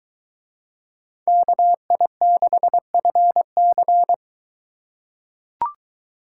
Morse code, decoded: KI6FC

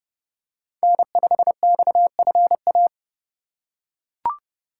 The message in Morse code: N5XFA

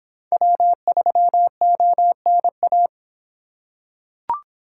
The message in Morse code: W3ONA

